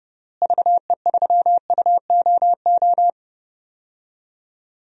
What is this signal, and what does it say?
Morse code, heard as a keyed tone, about 30 wpm, 700 Hz. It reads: VE3UOO
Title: DXCC-100 Call Signs - 6 Characters - Repeated 3x - 30wpm